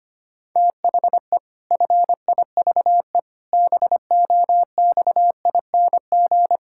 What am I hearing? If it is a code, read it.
Morse code, heard as a keyed tone, about 25 wpm, 700 Hz. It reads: THE FIVE BOXING
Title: Pangram Challenge v2 - 25wpm